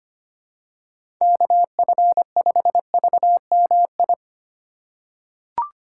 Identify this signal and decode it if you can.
Morse code, heard as a keyed tone, about 25 wpm, 700 Hz. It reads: KF5VMI
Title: USA Call Signs - 6 Characters - 2x Character Spacing - 25wpm